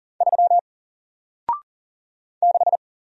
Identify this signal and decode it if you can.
Morse code, heard as a keyed tone, about 38 wpm, 700 Hz. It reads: 3 6
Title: Single Letter-Number Alt 40wpm